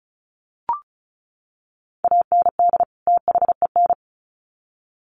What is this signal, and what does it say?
Morse code, heard as a keyed tone, about 34 wpm, 700 Hz. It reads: AND THEN
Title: Sets of 2 Words 35wpm